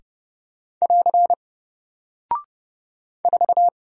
Morse code, heard as a keyed tone, about 31 wpm, 700 Hz. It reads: <AR> 4
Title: Single Character (Letter:Number:Punctuation:Prosign) 30wpm